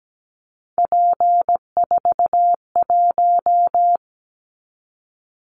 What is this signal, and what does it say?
Morse code, heard as a keyed tone, about 17 wpm, 700 Hz. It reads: P41
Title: Callsign Prefixes for Top 100 DXCC - 17wpm